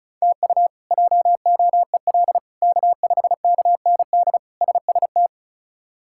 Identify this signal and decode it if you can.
Morse code, heard as a keyed tone, about 36 wpm, 700 Hz. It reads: TU JOEL K5KND SST
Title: K1USN SST QSO Practice - 35wpm